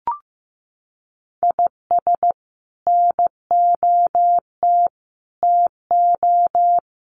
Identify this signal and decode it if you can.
Morse code, heard as a keyed tone, about 15 wpm, 700 Hz. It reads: IS NOT TO
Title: Sets of 5 Words 15wpm